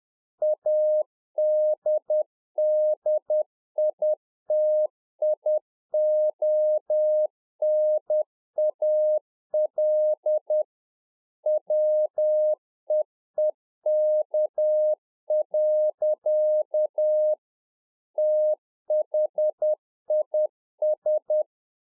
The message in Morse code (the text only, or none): ADDITIONAL WEEK. THIS